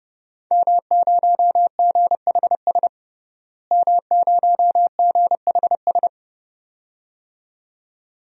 Morse code, heard as a keyed tone, about 30 wpm, 700 Hz. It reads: M0GHS M0GHS
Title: DXCC-100 Call Signs - 5 Characters - Repeated 2x - 30wpm